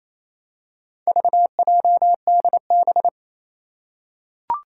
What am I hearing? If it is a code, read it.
Morse code, heard as a keyed tone, about 28 wpm, 700 Hz. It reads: VJDB
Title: Four Challenging Letter-Number Combinations 28wpm